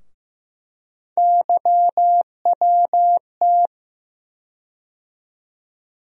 Morse code, heard as a keyed tone, about 15 wpm, 700 Hz. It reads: YWT
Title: USA Call Sign Postfixes - 3 Letters 15wpm